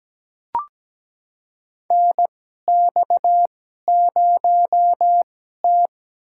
Morse code, heard as a keyed tone, about 17 wpm, spaced as for 12 wpm, 700 Hz. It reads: NX0T
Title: USA Call Signs - 4 Characters - 2x Character Spacing - 17wpm